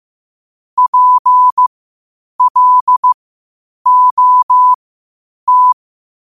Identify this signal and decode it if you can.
Morse code, heard as a keyed tone, about 15 wpm, spaced as for 8 wpm, 1000 Hz. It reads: PLOT
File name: qotdfiles / 8wpm